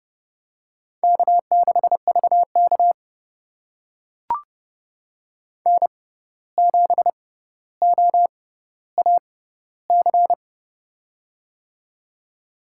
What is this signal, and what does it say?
Morse code, heard as a keyed tone, about 31 wpm, spaced as for 15 wpm, 700 Hz. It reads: K6VK N7OAC
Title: USA Call Signs - 6x Character Spacing - 30wpm